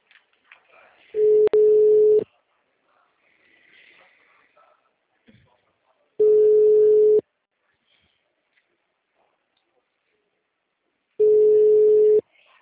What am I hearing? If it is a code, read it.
Ringback tone (Europe, 425 Hz).
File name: out- -2003- - - .26129